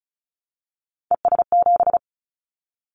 Morse code, heard as a keyed tone, about 36 wpm, 700 Hz. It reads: ES7
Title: Callsign Prefixes for Top 100 DXCC - 35wpm